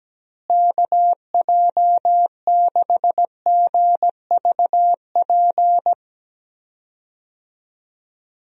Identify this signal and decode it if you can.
Morse code, heard as a keyed tone, about 17 wpm, 700 Hz. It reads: KJ6GVP
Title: USA Call Signs - 6 Characters - Repeated 3x - 17wpm